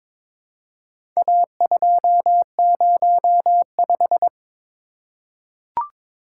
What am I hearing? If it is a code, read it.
Morse code, heard as a keyed tone, about 22 wpm, 700 Hz. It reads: A205
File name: practice / 22wpm / Four Letter-Number Combinations - 22wpm